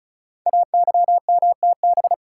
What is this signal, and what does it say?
Morse code, heard as a keyed tone, about 36 wpm, 700 Hz. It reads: AYMTB
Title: Code Groups - 5 Letters - 35wpm